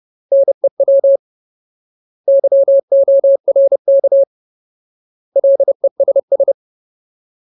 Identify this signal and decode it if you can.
Morse code, heard as a keyed tone, about 30 wpm, 550 Hz. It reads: NEW YORK LESS
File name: Ultimate Word List / 30WPM Ultimate